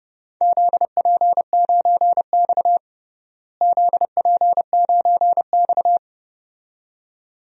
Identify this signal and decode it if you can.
Morse code, heard as a keyed tone, about 30 wpm, 700 Hz. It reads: ZP9X ZP9X
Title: DXCC-100 Call Signs - 4 Characters - Repeated 2x - 30wpm